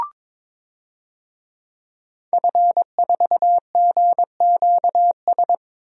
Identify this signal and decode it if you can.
Morse code, heard as a keyed tone, about 22 wpm, 700 Hz. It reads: F4GQS
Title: DXCC-100 Call Signs - 5 Characters - 22wpm